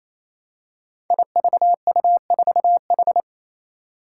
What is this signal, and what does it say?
Morse code, heard as a keyed tone, about 28 wpm, 700 Hz. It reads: IVU4H
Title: Five Challenging Letter-Number Combinations 28wpm